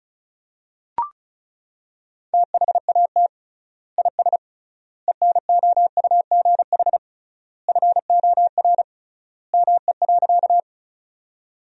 Morse code, heard as a keyed tone, about 34 wpm, 700 Hz. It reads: THAT IS ENOUGH FOR ME.
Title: 3x Word-Spacing - Sentences from Top 400 Words - 35wpm